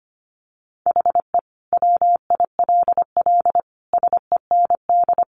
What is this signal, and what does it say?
Morse code, heard as a keyed tone, about 25 wpm, 700 Hz. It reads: HE WILL SEND
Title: Additional Easy Phrases 25wpm